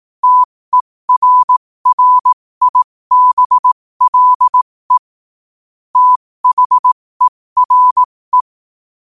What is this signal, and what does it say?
Morse code, heard as a keyed tone, about 18 wpm, spaced as for 15 wpm, 1000 Hz. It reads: TERRIBLE THERE
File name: qotdfiles / 15wpm